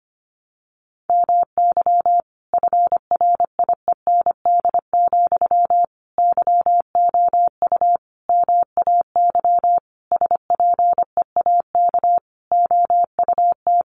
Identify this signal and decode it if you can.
Morse code, heard as a keyed tone, about 25 wpm, 700 Hz. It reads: MY FRIEND, YOU MAY SPEAK OUT
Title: Sentences from Top 1300 Words 25wpm